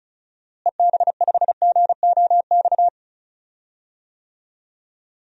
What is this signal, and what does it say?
Morse code, heard as a keyed tone, about 36 wpm, 700 Hz. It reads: EB5GOX
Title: DXCC-100 Call Signs - 6 Characters - Repeated 3x - 35wpm